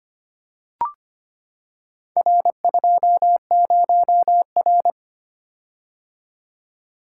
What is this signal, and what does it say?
Morse code, heard as a keyed tone, about 25 wpm, 700 Hz. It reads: R20R
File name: practice / 25wpm / Four Letter-Number Combinations - 25wpm